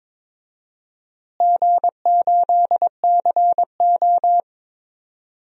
Morse code, heard as a keyed tone, about 22 wpm, 700 Hz. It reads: G8CO
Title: DXCC-100 Call Signs - 4 Characters - Repeated 2x - 22wpm